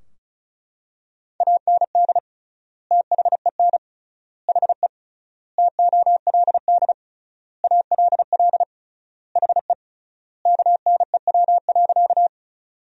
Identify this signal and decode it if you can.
Morse code, heard as a keyed tone, about 36 wpm, 700 Hz. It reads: AND THEN HE TOLD ALL HE KNEW.
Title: 3x Word-Spacing - Sentences from Top 400 Words - 35wpm